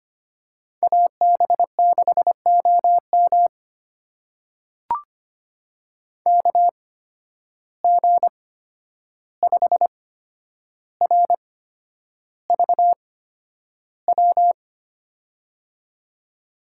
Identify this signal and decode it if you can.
Morse code, heard as a keyed tone, about 25 wpm, spaced as for 10 wpm, 700 Hz. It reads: AB6OM KG5RVW
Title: USA Call Signs - 8x Character Spacing - 25wpm